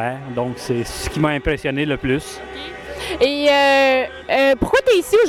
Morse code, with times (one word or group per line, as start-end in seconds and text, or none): none